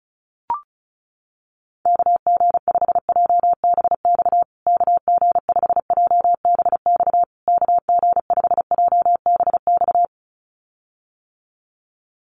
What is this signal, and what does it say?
Morse code, heard as a keyed tone, about 34 wpm, 700 Hz. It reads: KG5JBX KG5JBX KG5JBX
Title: USA Call Signs Repeated 3x 35wpm